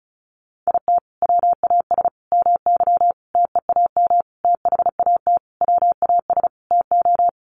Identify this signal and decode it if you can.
Morse code, heard as a keyed tone, about 34 wpm, 700 Hz. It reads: IT WAS MY TEAM THAT WAS TO
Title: Sentences from Top 1500 Words 35wpm